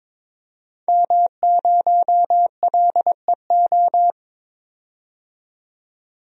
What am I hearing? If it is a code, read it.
Morse code, heard as a keyed tone, about 22 wpm, 700 Hz. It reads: M0LEO
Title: DXCC-100 Call Signs - 5 Characters - 22wpm